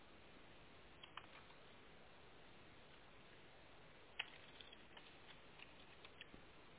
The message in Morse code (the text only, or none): none